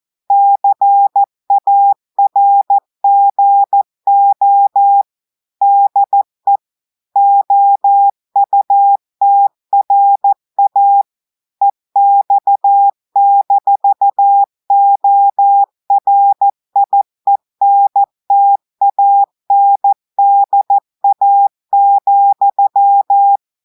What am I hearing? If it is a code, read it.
Morse code, heard as a keyed tone, about 14 wpm, 800 Hz. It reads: CARGO DE OUTRA EX-ORIENTANDA,
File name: audios / portugais / portugais texte en clair 1 vitesse 14 mots minutes